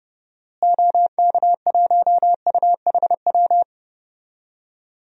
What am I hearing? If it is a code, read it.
Morse code, heard as a keyed tone, about 32 wpm, 700 Hz. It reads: OK1UHW
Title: DXCC-100 Call Signs - 6 Characters - Repeated 3x - 30wpm